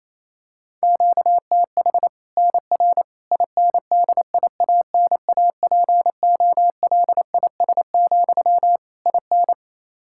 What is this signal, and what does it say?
Morse code, heard as a keyed tone, about 28 wpm, 700 Hz. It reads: QTH NR INDIANAPOLIS, IN
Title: Practice QSO - Exchange 1 - Beginning - 28wpm